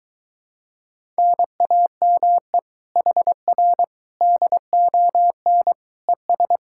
Morse code, heard as a keyed tone, about 23 wpm, 700 Hz. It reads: NAME HR DON ES